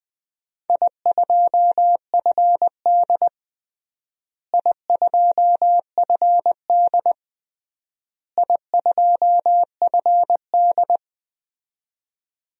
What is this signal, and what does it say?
Morse code, heard as a keyed tone, about 21 wpm, 700 Hz. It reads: I2FD I2FD I2FD
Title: DXCC-100 Call Signs - 4 Characters - Repeated 3x - 20wpm